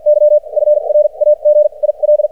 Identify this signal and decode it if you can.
Morse code, heard as a keyed tone, about 53 wpm, spaced as for 39 wpm, 600 Hz. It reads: Y FRAGIL
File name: archived / CRSnet